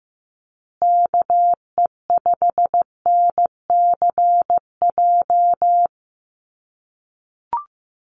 Morse code, heard as a keyed tone, about 14 wpm, 700 Hz. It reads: KE5NCJ
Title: USA Call Signs Repeated 3x 15wpm